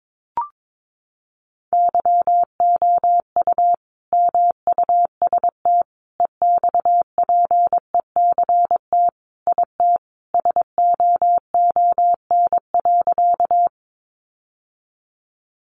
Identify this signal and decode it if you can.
Morse code, heard as a keyed tone, about 22 wpm, 700 Hz. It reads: YOU MUST EXPECT IT SOON.
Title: Sentences from Top 1100 Words 22wpm